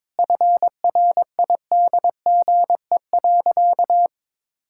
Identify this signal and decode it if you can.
Morse code, heard as a keyed tone, about 22 wpm, 690 Hz. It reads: FRIDGE.
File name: practice / 22wpm / Easy Sentences with Hints - 22wpm